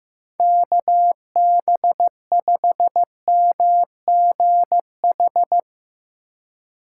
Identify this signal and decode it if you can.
Morse code, heard as a keyed tone, about 15 wpm, 700 Hz. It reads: KB5MGH
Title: USA Call Signs - 3x Character Spacing - 15wpm